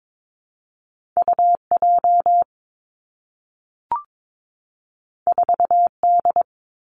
Morse code, heard as a keyed tone, about 22 wpm, 700 Hz. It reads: U J 4 D
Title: Two Challenging Letter-Number Combinations 22wpm